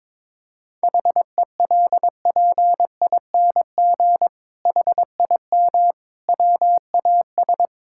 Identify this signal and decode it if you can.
Morse code, heard as a keyed tone, about 22 wpm, 700 Hz. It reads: HELPING HIM WAS
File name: Sentences with 7-Word Phrases - 22wpm